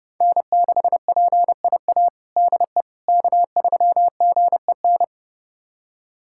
Morse code, heard as a keyed tone, about 32 wpm, 700 Hz. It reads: N6PIA DE K3GEN
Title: USA Callsign DE Callsign - 5 Characters - 30wpm